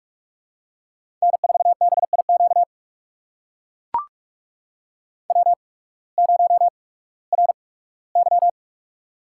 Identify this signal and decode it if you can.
Morse code, heard as a keyed tone, about 44 wpm, spaced as for 21 wpm, 700 Hz. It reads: N4BIQ W0RY
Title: USA Call Signs - 8x Character Spacing - 45wpm